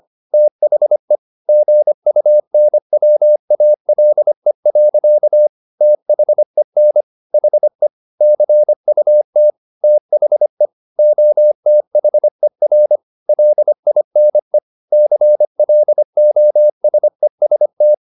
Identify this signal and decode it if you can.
Morse code, heard as a keyed tone, about 25 wpm, 600 Hz. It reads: THE GUNWALE. THEN HE CUT THE OTHER LINE CLOSEST
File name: the old man and the sea part 1 25wpm